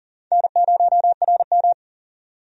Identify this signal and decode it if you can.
Morse code, heard as a keyed tone, about 40 wpm, 700 Hz. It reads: N0RM
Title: USA Call Signs - 4 Characters - 40wpm